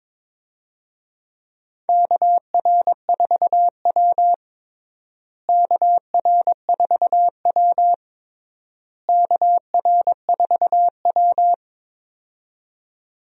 Morse code, heard as a keyed tone, about 22 wpm, 700 Hz. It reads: KR4W KR4W KR4W
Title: USA Call Signs - 4 Characters - Repeated 3x - 22wpm